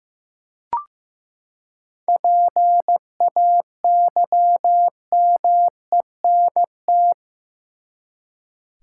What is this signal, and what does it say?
Morse code, heard as a keyed tone, about 15 wpm, 700 Hz. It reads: PAYMENT
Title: Seven Letter Words - 15wpm